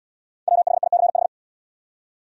Morse code, heard as a keyed tone, about 70 wpm, 700 Hz. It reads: WHICH